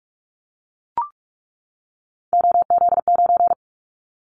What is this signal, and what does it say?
Morse code, heard as a keyed tone, about 44 wpm, 700 Hz. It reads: OZ9
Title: Callsign Prefixes for Top 100 DXCC - 45wpm